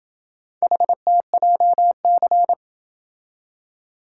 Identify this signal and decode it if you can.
Morse code, heard as a keyed tone, about 26 wpm, 700 Hz. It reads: HTJC